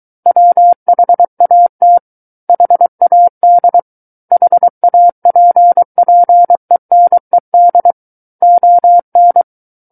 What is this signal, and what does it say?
Morse code, heard as a keyed tone, about 23 wpm, 700 Hz. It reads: WHAT HAD HAPPENED ON
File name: Book One / 14-Chapter 14 - 23 WPM